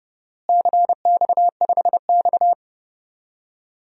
Morse code, heard as a keyed tone, about 30 wpm, 700 Hz. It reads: CX5X
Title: DXCC-100 Call Signs - 4 Characters - Repeated 2x - 30wpm